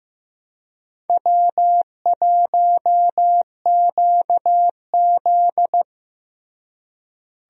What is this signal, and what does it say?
Morse code, heard as a keyed tone, about 15 wpm, 700 Hz. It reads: W1QZ
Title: USA Call Signs - 4 Characters - Repeated 2x - 15wpm